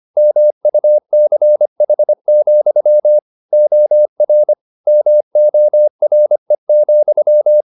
Morse code, heard as a keyed tone, about 25 wpm, 600 Hz. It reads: MUCH, OR MORE,